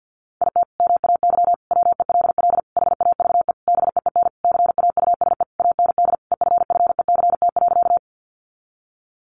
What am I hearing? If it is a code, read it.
Morse code, heard as a keyed tone, about 50 wpm, 700 Hz. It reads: IT MAY WELL HAVE BEEN CAUSE AND EFFECT.
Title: Sentences from Top 1000 Words 50wpm